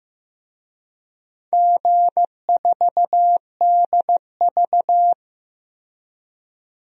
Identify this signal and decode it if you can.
Morse code, heard as a keyed tone, about 15 wpm, 700 Hz. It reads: G4DV
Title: DXCC-100 Call Signs - 4 Characters - 15wpm